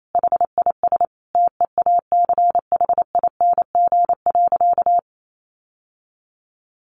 Morse code, heard as a keyed tone, about 28 wpm, 700 Hz. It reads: HIS TEACHING.
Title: Sentences from Top 1100 Words 28wpm